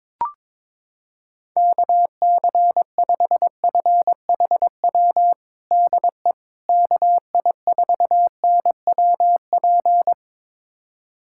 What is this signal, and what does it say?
Morse code, heard as a keyed tone, about 22 wpm, 700 Hz. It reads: KC5FHW DE KI4NWP
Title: USA Callsign DE Callsign 22wpm